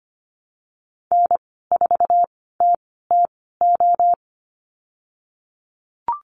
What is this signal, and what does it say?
Morse code, heard as a keyed tone, about 25 wpm, spaced as for 16 wpm, 700 Hz. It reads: N4TTO
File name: Lesson 15g - Number 5 - Call Signs 25wpm